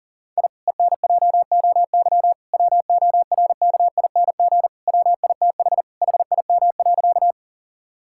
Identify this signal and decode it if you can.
Morse code, heard as a keyed tone, about 40 wpm, 700 Hz. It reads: I ENJOY WORKING WITH HIM.
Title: Sentences from Top 1000 Words - Review Part 5 - 40wpm